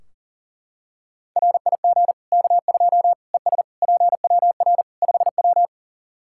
Morse code, heard as a keyed tone, about 40 wpm, 700 Hz. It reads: RIG K2 ES PWR 5W
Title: Practice QSO - Exchange 2 - Info - 40wpm